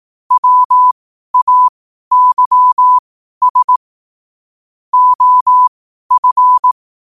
Morse code, heard as a keyed tone, about 18 wpm, spaced as for 12 wpm, 1000 Hz. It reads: WAYS OF
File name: qotdfiles / 12wpm